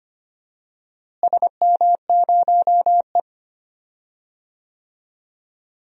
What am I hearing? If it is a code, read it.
Morse code, heard as a keyed tone, about 25 wpm, 700 Hz. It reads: SM0E
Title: DXCC-100 Call Signs - 4 Characters - Repeated 3x - 25wpm